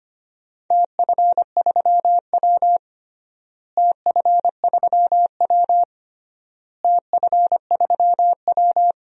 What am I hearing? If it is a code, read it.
Morse code, heard as a keyed tone, about 25 wpm, 700 Hz. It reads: TF3W TF3W TF3W
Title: DXCC-100 Call Signs - 4 Characters - Repeated 3x - 25wpm